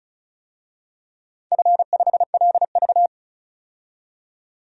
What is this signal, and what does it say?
Morse code, heard as a keyed tone, about 36 wpm, 700 Hz. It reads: F5LV
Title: DXCC-100 Call Signs - 4 Characters - 35wpm